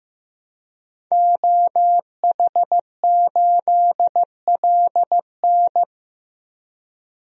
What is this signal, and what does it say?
Morse code, heard as a keyed tone, about 15 wpm, 700 Hz. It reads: OH8LN
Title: DXCC-100 Call Signs - 5 Characters - 15wpm